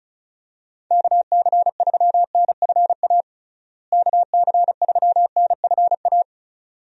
Morse code, heard as a keyed tone, about 36 wpm, 700 Hz. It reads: KC3NFA KC3NFA
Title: USA Call Signs - 6 Characters - Repeated 2x - 35wpm